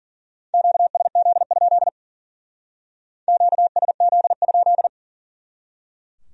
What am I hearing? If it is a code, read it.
Morse code, heard as a keyed tone, about 48 wpm, 700 Hz. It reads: QSZ? QSZ?
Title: Speed-Racing Q-Codes - 40wpm